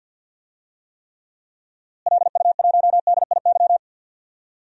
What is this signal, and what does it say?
Morse code, heard as a keyed tone, about 50 wpm, 700 Hz. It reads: LU1DIY
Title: DXCC-100 Call Signs - 6 Characters - Repeated 3x - 50wpm